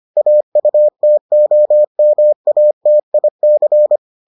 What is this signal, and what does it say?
Morse code, heard as a keyed tone, about 25 wpm, 600 Hz. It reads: AUTOMATIC